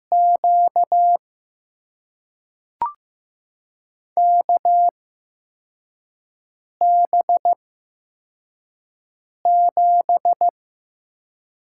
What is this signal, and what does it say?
Morse code, heard as a keyed tone, about 15 wpm, spaced as for 4 wpm, 700 Hz. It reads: Q KB7
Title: USA Call Signs - 6 Characters - 8x Character Spacing - 15wpm